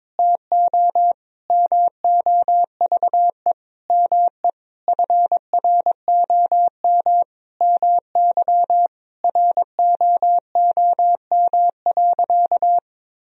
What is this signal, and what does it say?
Morse code, heard as a keyed tone, about 22 wpm, 700 Hz. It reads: TO MOVE ME FROM MY ROOM.
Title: Sentences from Top 1000 Words - Review Part 1 - 22wpm